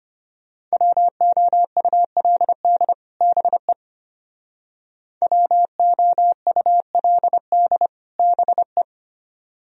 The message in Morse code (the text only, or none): WOULD BE WOULD BE